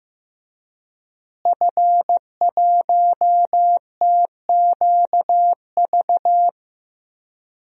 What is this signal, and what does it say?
Morse code, heard as a keyed tone, about 15 wpm, 700 Hz. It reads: F1TQV
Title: DXCC-100 Call Signs - 5 Characters - Repeated 2x - 15wpm